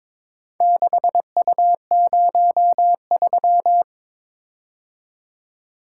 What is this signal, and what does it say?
Morse code, heard as a keyed tone, about 22 wpm, 700 Hz. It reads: 6U03
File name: practice / 22wpm / Four Letter-Number Combinations - 22wpm